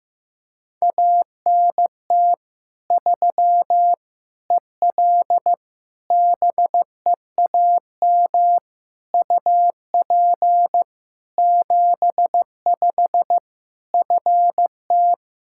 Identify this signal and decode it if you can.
Morse code, heard as a keyed tone, about 15 wpm, 700 Hz. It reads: ANT 3 EL BEAM UP 75 FT
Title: Practice QSO - Exchange 2 - Info - 15wpm